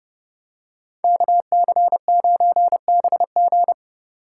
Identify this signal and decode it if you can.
Morse code, heard as a keyed tone, about 30 wpm, 700 Hz. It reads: KC9BG
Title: K1USN SST QSO Practice - 30wpm